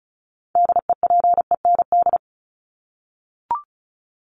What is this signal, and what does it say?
Morse code, heard as a keyed tone, about 34 wpm, 700 Hz. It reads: DEPEND
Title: ING Suffix - Top 100 Words with Base Words 35wpm